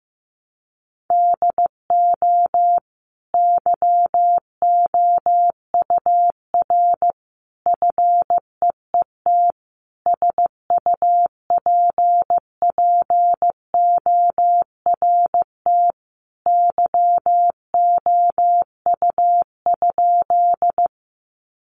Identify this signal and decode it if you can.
Morse code, heard as a keyed tone, about 15 wpm, 700 Hz. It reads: DO YOUR FEET SUPPORT YOU?
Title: Sentences from Top 800 Words 15wpm